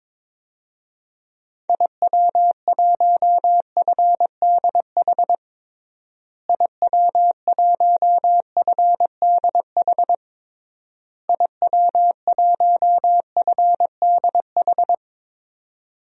Morse code, heard as a keyed tone, about 22 wpm, 700 Hz. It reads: IW1FDH IW1FDH IW1FDH
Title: DXCC-100 Call Signs - 6 Characters - Repeated 3x - 22wpm